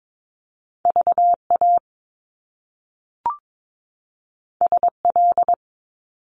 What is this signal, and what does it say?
Morse code, heard as a keyed tone, about 22 wpm, 700 Hz. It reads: VA SL
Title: Top Two Letter Combinations in Words 22wpm